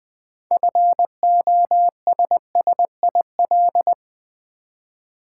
Morse code, heard as a keyed tone, about 20 wpm, 700 Hz. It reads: FOSSIL
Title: 6 Letter Words - First 3 Letters Given - 20wpm